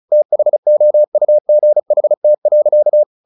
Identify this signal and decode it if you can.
Morse code, heard as a keyed tone, about 34 wpm, 600 Hz. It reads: THOUGHT.
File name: the old man and the sea / the old man and the sea 1 35wpm